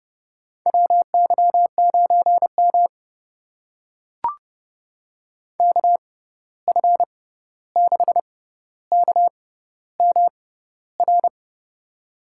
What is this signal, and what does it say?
Morse code, heard as a keyed tone, about 30 wpm, spaced as for 14 wpm, 700 Hz. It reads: WY9M KF6KMR
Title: USA Call Signs - 6x Character Spacing - 30wpm